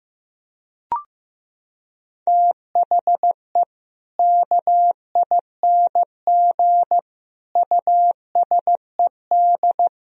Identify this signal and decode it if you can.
Morse code, heard as a keyed tone, about 16 wpm, 700 Hz. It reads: THE KING USED
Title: Sentences from Top 1000 Words - Review Part 4 - 15wpm